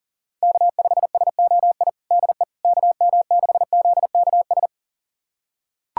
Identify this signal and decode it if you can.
Morse code, heard as a keyed tone, about 37 wpm, 700 Hz. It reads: K5SOI DE KM6ZKS
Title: USA Callsign DE Callsign 40wpm